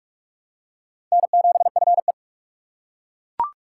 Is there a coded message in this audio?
Yes: Morse code from 1.1 to 2.1 s, heard as a keyed tone, about 38 wpm, 700 Hz. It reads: N7FE